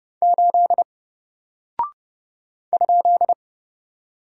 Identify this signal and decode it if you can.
Morse code, heard as a keyed tone, about 30 wpm, 700 Hz. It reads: 8 ?
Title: Single Character (Letter:Number:Punctuation:Prosign) Alt 30wpm